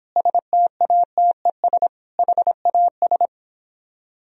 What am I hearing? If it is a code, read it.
Morse code, heard as a keyed tone, about 31 wpm, 700 Hz. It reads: STATES HAS